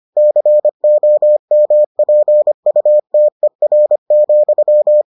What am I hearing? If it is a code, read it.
Morse code, heard as a keyed tone, about 25 wpm, 600 Hz. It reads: COMPUTER,